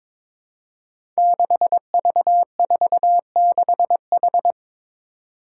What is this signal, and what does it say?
Morse code, heard as a keyed tone, about 22 wpm, 700 Hz. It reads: 6V46H
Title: Five Challenging Letter-Number Combinations 22wpm